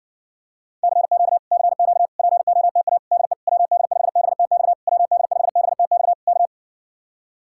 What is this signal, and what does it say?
Morse code, heard as a keyed tone, about 60 wpm, 700 Hz. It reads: CQ CQ POTA DE WD5BTX WD5BTX K